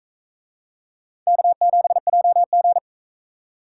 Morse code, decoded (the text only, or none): K7JG